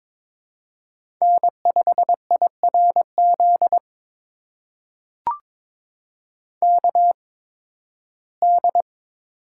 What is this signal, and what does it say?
Morse code, heard as a keyed tone, about 22 wpm, spaced as for 14 wpm, 700 Hz. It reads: N5IRZ KD
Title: USA Call Signs - 5 Characters - 8x Character Spacing - 22wpm